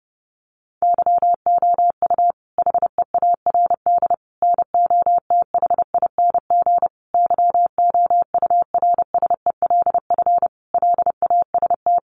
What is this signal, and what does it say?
Morse code, heard as a keyed tone, about 30 wpm, 700 Hz. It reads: YOU HEARD NOTHING YOURSELF LAST
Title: Sentences from Top 600 Words 30wpm